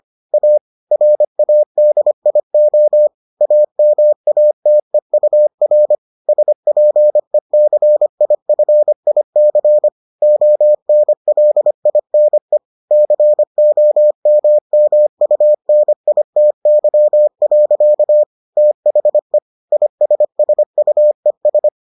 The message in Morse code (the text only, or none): A RADIO AMATEUR SPECIFIC ONLINE COMMUNITY. THE ISSUES